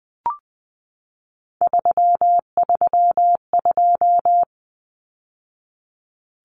Morse code, heard as a keyed tone, about 20 wpm, 700 Hz. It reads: 332
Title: Numbers - 3-Digit 20wpm